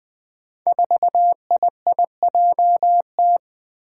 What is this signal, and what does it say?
Morse code, heard as a keyed tone, about 20 wpm, 700 Hz. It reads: 4IIJT
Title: Five Challenging Letter-Number Combinations 20wpm